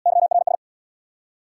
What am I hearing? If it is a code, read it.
Morse code, heard as a keyed tone, about 77 wpm, 700 Hz. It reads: YRS